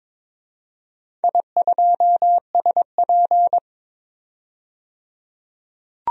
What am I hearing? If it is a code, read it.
Morse code, heard as a keyed tone, about 23 wpm, 700 Hz. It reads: I2SP
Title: DXCC-100 Call Signs - 4 Characters - 22wpm